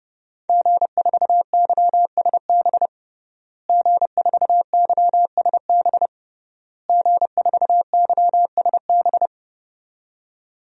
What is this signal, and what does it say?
Morse code, heard as a keyed tone, about 29 wpm, 700 Hz. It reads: G4YSB G4YSB G4YSB
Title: DXCC-100 Call Signs - 5 Characters - Repeated 3x - 30wpm